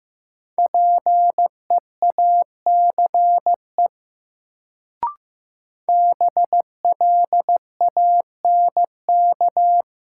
Morse code, heard as a keyed tone, about 15 wpm, 700 Hz. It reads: PEACE BLANK
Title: Sentences from Top 1500 Words - New Word Drill Alt - 15wpm